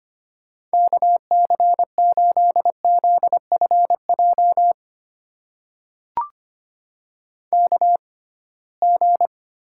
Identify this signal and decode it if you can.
Morse code, heard as a keyed tone, about 25 wpm, spaced as for 19 wpm, 700 Hz. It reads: KC8ZFJ KG
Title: USA Call Signs - 6 Characters - 6x Character Spacing - 25wpm